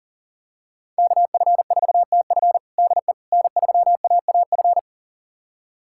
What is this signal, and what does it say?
Morse code, heard as a keyed tone, about 40 wpm, 700 Hz. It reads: KF4TF DE N3AAF
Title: USA Callsign DE Callsign - 5 Characters - 40wpm